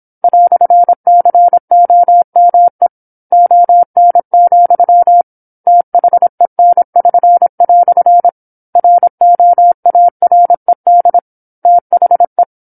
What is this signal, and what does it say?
Morse code, heard as a keyed tone, about 27 wpm, 700 Hz. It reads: "COME ON, THEN<SN>" ROARED THE
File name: 08-Chapter-08 26 WPM